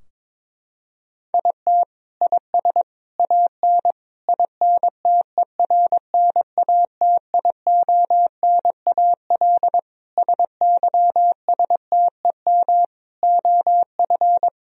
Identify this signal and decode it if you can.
Morse code, heard as a keyed tone, about 22 wpm, 700 Hz. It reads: IT IS AN INTERNATIONAL SYSTEM OF
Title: Sentences from Top 1000 Words - Review Part 4 - 22wpm